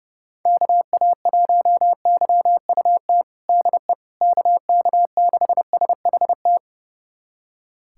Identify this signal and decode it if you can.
Morse code, heard as a keyed tone, about 30 wpm, 700 Hz. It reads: KA1YUT DE KK6SHT
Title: USA Callsign DE Callsign 30wpm